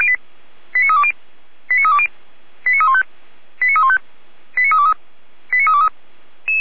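Five-tone selective call (ZVEI), heard as quick runs of five tones, tones of about 70 ms. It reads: 89320 89320 89315 89315 89323 89323
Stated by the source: ZVEI five-tone selective call received by SDR.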